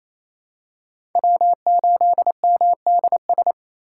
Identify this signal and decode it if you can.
Morse code, heard as a keyed tone, about 28 wpm, 700 Hz. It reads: W8MDS
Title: USA Call Signs - 6x Character Spacing - 28wpm